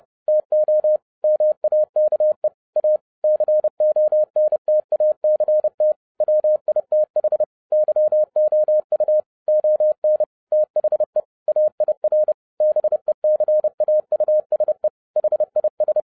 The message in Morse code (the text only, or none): TO MAKE A CONTACT WITH YOU ON THE AIR BECAUSE HIS